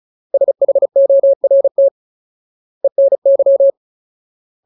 Morse code, heard as a keyed tone, about 35 wpm, 550 Hz. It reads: SHORT ENY
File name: Ultimate Word List / 35WPM Ultimate